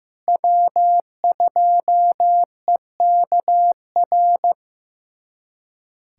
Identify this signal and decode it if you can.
Morse code, heard as a keyed tone, about 15 wpm, 700 Hz. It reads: W2EKR
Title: USA Callsign DE Callsign - 5 Characters - 15wpm